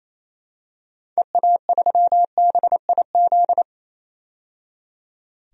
Morse code, heard as a keyed tone, about 28 wpm, 700 Hz. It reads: EA3BIZ
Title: DXCC-100 Call Signs - 6 Characters - 28wpm